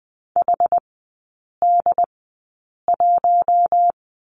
Morse code, heard as a keyed tone, about 20 wpm, spaced as for 8 wpm, 700 Hz. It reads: HD1
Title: 067 Lesson 12e - Letter d - Three Characters 20wpm